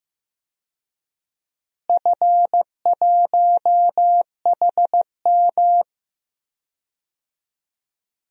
Morse code, heard as a keyed tone, about 15 wpm, 700 Hz. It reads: F1HM
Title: DXCC-100 Call Signs - 4 Characters - Repeated 2x - 15wpm